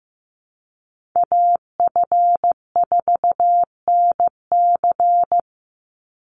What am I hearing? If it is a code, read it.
Morse code, heard as a keyed tone, about 15 wpm, 700 Hz. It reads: AF4NC
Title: Call Signs Repeated 3x - 15wpm